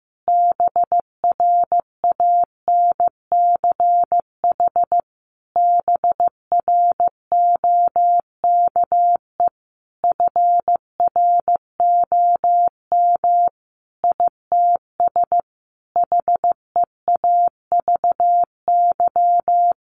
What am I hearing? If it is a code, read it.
Morse code, heard as a keyed tone, about 14 wpm, 700 Hz. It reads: BRANCH BROKE FROM ITS HEAVY
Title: A Sound Of Thunder 15wpm